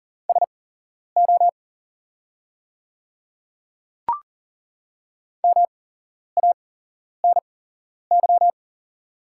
Morse code, heard as a keyed tone, about 40 wpm, spaced as for 11 wpm, 700 Hz. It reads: SO MANY
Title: 8x Character Spacing - Top 100 Words - 40wpm